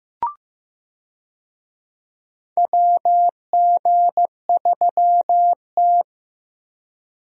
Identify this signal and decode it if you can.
Morse code, heard as a keyed tone, about 15 wpm, 700 Hz. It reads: WG3T
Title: USA Call Signs - 4 Characters - Repeated 2x - 15wpm